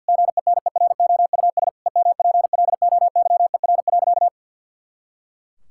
Morse code, heard as a keyed tone, about 50 wpm, 700 Hz. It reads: GENEROUS EMPLOYER.
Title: Pangram Challenge v2 - 50wpm